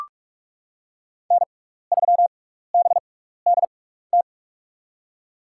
Morse code, heard as a keyed tone, about 45 wpm, spaced as for 15 wpm, 690 Hz. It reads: N3BDT
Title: USA Call Signs - 6x Character Spacing - 45wpm